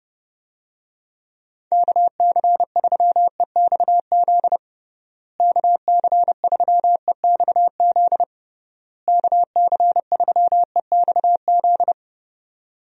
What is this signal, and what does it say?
Morse code, heard as a keyed tone, about 30 wpm, 700 Hz. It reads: KC3EXZ KC3EXZ KC3EXZ
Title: USA Call Signs - 6 Characters - Repeated 3x - 30wpm